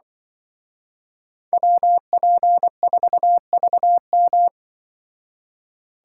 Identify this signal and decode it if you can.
Morse code, heard as a keyed tone, about 23 wpm, 700 Hz. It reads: WP4VM
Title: Speed Racing - USA 2x2 Call Signs 20wpm